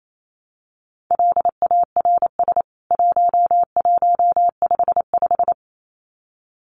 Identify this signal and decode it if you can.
Morse code, heard as a keyed tone, about 28 wpm, 700 Hz. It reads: LARS 1155
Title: CWT Exchange 28wpm